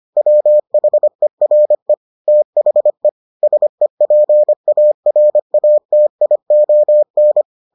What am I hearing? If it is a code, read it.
Morse code, heard as a keyed tone, about 25 wpm, 600 Hz. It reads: WHERE THE SEPARATION